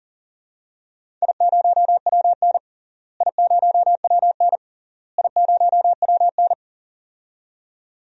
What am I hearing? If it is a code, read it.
Morse code, heard as a keyed tone, about 40 wpm, 700 Hz. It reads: I0WN I0WN I0WN
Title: DXCC-100 Call Signs - 4 Characters - Repeated 3x - 40wpm